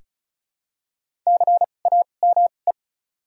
Morse code, heard as a keyed tone, about 34 wpm, spaced as for 25 wpm, 700 Hz. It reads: CAME
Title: 2x Character Spacing - Top 200 Words - 35wpm